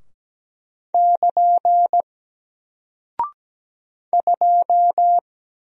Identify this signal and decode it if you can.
Morse code, heard as a keyed tone, about 17 wpm, 700 Hz. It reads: <KN> 2
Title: Single Character (Letter:Number:Punctuation:Prosign) 17wpm